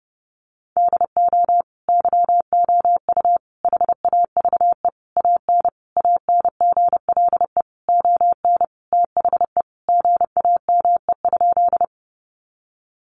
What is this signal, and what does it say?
Morse code, heard as a keyed tone, about 30 wpm, 700 Hz. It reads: DO YOU HAVE AN ANGLE ON THE GAME?
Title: Sentences from Top 1500 Words 30wpm